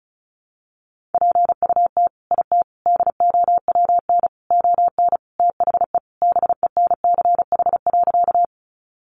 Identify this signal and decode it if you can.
Morse code, heard as a keyed tone, about 34 wpm, 700 Hz. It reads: PUT IT DOWN ON THE BENCH.
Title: Sentences from Top 1400 Words 35wpm